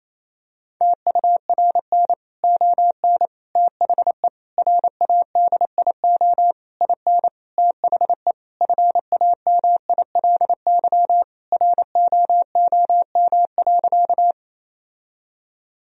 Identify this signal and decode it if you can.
Morse code, heard as a keyed tone, about 28 wpm, 700 Hz. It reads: TURN ON THE RADIO IN THE FAMILY ROOM.
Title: Sentences from Top 1000 Words - Review Part 3 - 28wpm